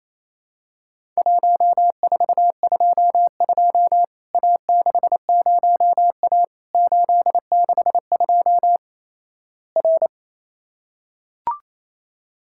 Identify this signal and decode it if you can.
Morse code, heard as a keyed tone, about 27 wpm, 700 Hz. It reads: 1422 A60A 862 R
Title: Worked All Europe DX Contest - QTC Exchange - 28wpm